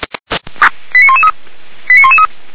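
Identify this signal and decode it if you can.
Five-tone selective call (ZVEI), heard as quick runs of five tones, tones of about 70 ms. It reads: 89113 89113
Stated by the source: ZVEI five-tone selective call received on an SDR.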